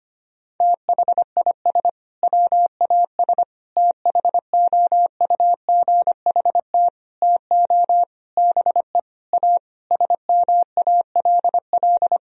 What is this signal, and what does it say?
Morse code, heard as a keyed tone, about 25 wpm, 700 Hz. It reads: THIS WAS THOUGHT TO BE A SMALL
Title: Sentences from Top 1000 Words - Review Part 1 - 25wpm